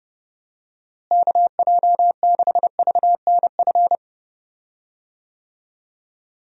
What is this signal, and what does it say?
Morse code, heard as a keyed tone, about 30 wpm, 700 Hz. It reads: KJ6VNF
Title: USA Call Signs - 6 Characters - Repeated 3x - 30wpm